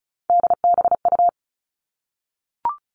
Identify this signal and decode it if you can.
Morse code, heard as a keyed tone, about 34 wpm, 700 Hz. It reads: DBU
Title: All Cut Numbers - 3-Digit 35wpm